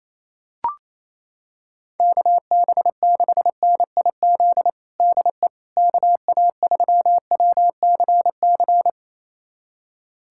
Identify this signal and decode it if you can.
Morse code, heard as a keyed tone, about 28 wpm, 700 Hz. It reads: KB6NIZ DE KA3WCC
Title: USA Callsign DE Callsign - 6 Characters - 28wpm